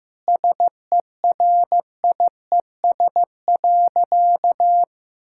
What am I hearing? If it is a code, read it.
Morse code, heard as a keyed tone, about 15 wpm, 700 Hz. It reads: SERIES.
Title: Sentences from Top 1000 Words - Review Part 5 - 15wpm